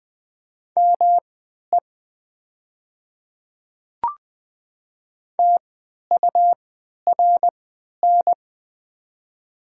Morse code, heard as a keyed tone, about 20 wpm, spaced as for 11 wpm, 700 Hz. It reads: ME TURN
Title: 3x Character Spacing - Top 200 Words - 20wpm